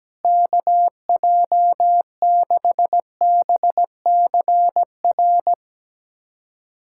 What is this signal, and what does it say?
Morse code, heard as a keyed tone, about 17 wpm, 700 Hz. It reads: KJ6BCR
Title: USA Callsign DE Callsign - 6 Characters - 17wpm